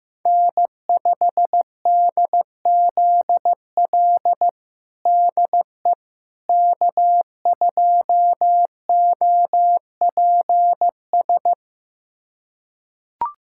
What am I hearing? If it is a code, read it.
Morse code, heard as a keyed tone, about 15 wpm, 700 Hz. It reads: N5DZL DE K2OPS
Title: USA Callsign DE Callsign - 5 Characters - 15wpm